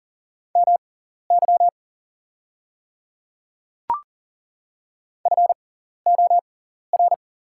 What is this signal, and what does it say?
Morse code, heard as a keyed tone, about 40 wpm, spaced as for 14 wpm, 690 Hz. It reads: MY FOR
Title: 6x Character Spacing - Top 100 Words - 40wpm